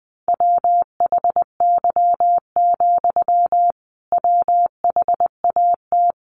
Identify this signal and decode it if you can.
Morse code, heard as a keyed tone, about 20 wpm, 700 Hz. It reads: WHY, WHAT
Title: Sentences from Top 200 Words 20wpm